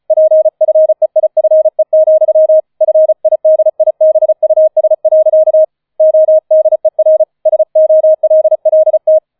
Morse code, heard as a keyed tone, about 36 wpm, 630 Hz. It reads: PFEIFE, FIDIBUS. ODER SOLLT
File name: 005 Fünfter Streich 35wpm